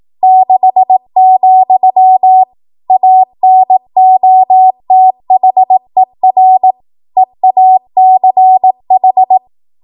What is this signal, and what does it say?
Morse code, heard as a keyed tone, about 18 wpm, 750 Hz. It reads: 6, ANOTHER EACH
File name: Archive / 18 WPM / 18WPM